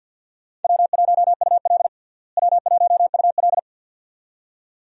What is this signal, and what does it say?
Morse code, heard as a keyed tone, about 52 wpm, 700 Hz. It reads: W1UL W1UL
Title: USA Call Signs - 4 Characters - Repeated 2x - 50wpm